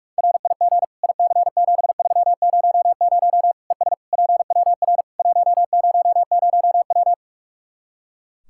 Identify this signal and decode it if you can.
Morse code, heard as a keyed tone, about 46 wpm, 700 Hz. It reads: RIG IC7300 ES PWR 100W
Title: Practice QSO - Exchange 2 - Info - 45wpm